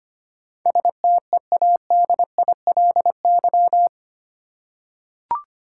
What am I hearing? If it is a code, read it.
Morse code, heard as a keyed tone, about 25 wpm, 700 Hz. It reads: STEADILY
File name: 8 Letter Words - First 3 Letters Given - 25wpm